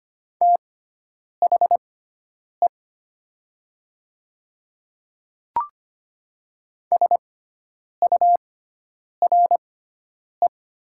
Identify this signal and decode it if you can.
Morse code, heard as a keyed tone, about 25 wpm, spaced as for 9 wpm, 700 Hz. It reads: THE SURE
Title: 6x Character Spacing - Top 300 Words - 25wpm